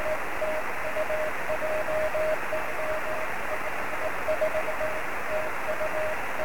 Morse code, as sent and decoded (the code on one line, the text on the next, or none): none